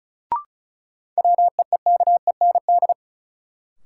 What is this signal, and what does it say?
Morse code, heard as a keyed tone, about 36 wpm, 700 Hz. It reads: WEEKEND
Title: Sets of 5 Words - New Word Drill - 35wpm